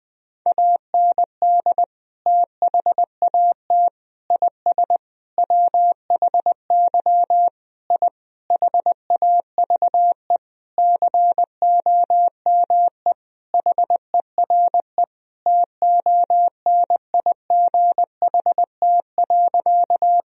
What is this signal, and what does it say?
Morse code, heard as a keyed tone, about 21 wpm, 700 Hz. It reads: AND THAT IS WHY I HAVE COME HERE TONIGHT.
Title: Sentences from Top 1000 Words - Review Part 3 - 20wpm